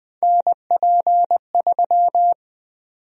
Morse code, heard as a keyed tone, about 21 wpm, 700 Hz. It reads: NP3
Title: USA Call Sign Prefixes - 20wpm